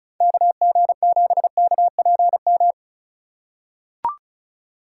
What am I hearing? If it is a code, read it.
Morse code, heard as a keyed tone, about 34 wpm, 700 Hz. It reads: KG7KPM
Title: USA Call Signs - 6 Characters - 2x Character Spacing - 35wpm